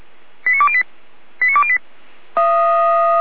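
Five-tone selective call (ZVEI), heard as quick runs of five tones, tones of about 70 ms. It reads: 89208 89208
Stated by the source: ZVEI five-tone call 89285 received by SDR.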